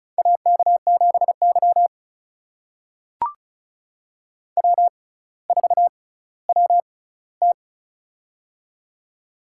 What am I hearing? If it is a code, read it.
Morse code, heard as a keyed tone, about 36 wpm, spaced as for 18 wpm, 700 Hz. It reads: AK7Y W4WT
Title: USA Call Signs - 4 Characters - 6x Character Spacing - 35wpm